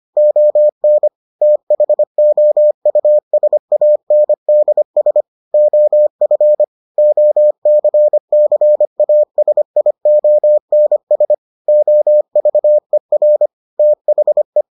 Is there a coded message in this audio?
Yes: Morse code from 0.2 to 14.6 s, heard as a keyed tone, about 25 wpm, 600 Hz. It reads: ON THOUSANDS OF OCCASIONS OVER THE